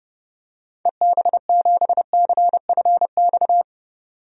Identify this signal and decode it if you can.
Morse code, heard as a keyed tone, about 32 wpm, 700 Hz. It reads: EB7CFX
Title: DXCC-100 Call Signs - 6 Characters - 30wpm